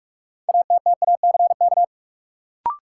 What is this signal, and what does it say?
Morse code, heard as a keyed tone, about 43 wpm, 700 Hz. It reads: ATTACK
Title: Sentences from Top 1500 Words - New Word Drill Alt - 45wpm